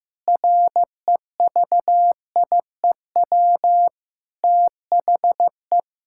Morse code, heard as a keyed tone, about 15 wpm, 700 Hz. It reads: REVIEW THE
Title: Sentences from Top 1000 Words - Review Part 6 - 15wpm